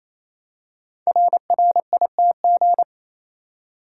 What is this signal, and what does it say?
Morse code, heard as a keyed tone, about 27 wpm, 700 Hz. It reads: RRITG
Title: Code Groups - 5 Letter:Numbers - 28wpm